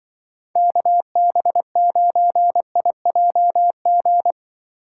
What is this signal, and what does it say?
Morse code, heard as a keyed tone, about 24 wpm, 700 Hz. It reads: KB9IJG
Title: Speed Racing - USA 2x3 Call Signs 20wpm